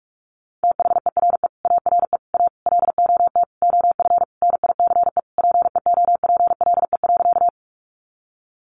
Morse code, heard as a keyed tone, about 44 wpm, 700 Hz. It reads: THERE ARE A LOT OF NICE PEOPLE.
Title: Sentences from Top 1100 Words 45wpm